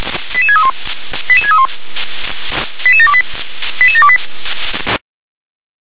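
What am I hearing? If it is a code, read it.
Five-tone selective call (ZVEI), heard as quick runs of five tones, tones of about 70 ms. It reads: 88631 88631 88728 88728
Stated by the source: ZVEI five-tone selective call 88740 received with an SDR.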